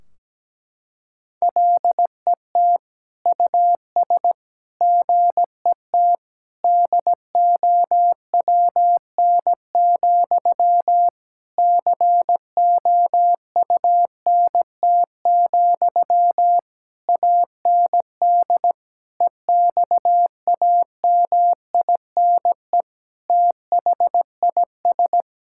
Morse code, heard as a keyed tone, about 17 wpm, 700 Hz. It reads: LET US GET DOWN, COUNT, AND EXAMINE THIS
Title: Sentences from Top 1600 Words 17wpm